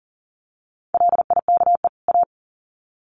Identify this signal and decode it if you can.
Morse code, heard as a keyed tone, about 40 wpm, 700 Hz. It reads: LIKE A
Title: Sets of 2 Words 40wpm